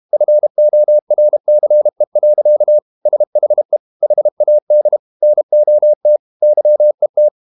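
Morse code, heard as a keyed tone, about 32 wpm, 600 Hz. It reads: FORCE. SHE HAD NOT YET
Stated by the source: Morse code practice at 32 wpm from CW med Gnister, reading Ken Follett's novel Never in English.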